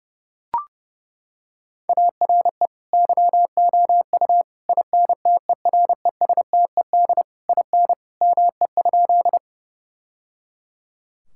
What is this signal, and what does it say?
Morse code, heard as a keyed tone, about 30 wpm, 700 Hz. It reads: ARE YOU INTERESTED IN ME?
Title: Sentences from Top 1000 Words - Review Part 6 - 30wpm